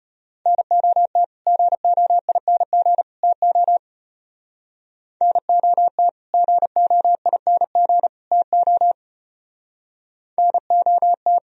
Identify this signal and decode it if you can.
Morse code, heard as a keyed tone, about 32 wpm, 700 Hz. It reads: NOT GOING TO NOT GOING TO NOT